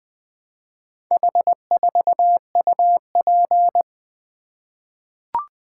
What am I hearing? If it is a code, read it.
Morse code, heard as a keyed tone, about 20 wpm, 700 Hz. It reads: H4UP
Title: Four Letter-Number Combinations - 20wpm